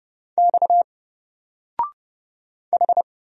Morse code, heard as a keyed tone, about 30 wpm, 700 Hz. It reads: X H
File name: Single-Letter-Number-30wpm